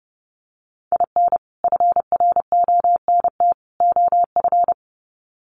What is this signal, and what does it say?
Morse code, heard as a keyed tone, about 30 wpm, 700 Hz. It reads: IN FRONT OF
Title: Sets of 3 Words 30wpm